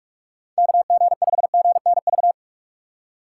Morse code, heard as a keyed tone, about 42 wpm, 690 Hz. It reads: KG5GNV
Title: USA Call Signs - 6 Characters - 3x Character Spacing - 45wpm